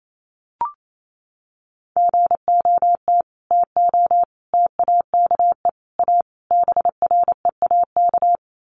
Morse code, heard as a keyed tone, about 28 wpm, 700 Hz. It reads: GOT TO TAKE A BREAK
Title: Sets of 5 Words Encore 28wpm